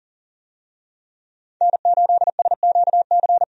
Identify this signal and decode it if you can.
Morse code, heard as a keyed tone, about 40 wpm, 700 Hz. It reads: N8SQC
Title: USA Call Signs - 5 Characters - 40wpm